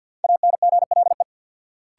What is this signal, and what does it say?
Morse code, heard as a keyed tone, about 53 wpm, 700 Hz. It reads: ANGLE
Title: Sentences from Top 1500 Words - New Word Drill - 50wpm